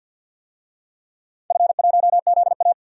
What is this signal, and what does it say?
Morse code, heard as a keyed tone, about 48 wpm, 700 Hz. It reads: F1ZA
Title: DXCC-100 Call Signs - 4 Characters - 50wpm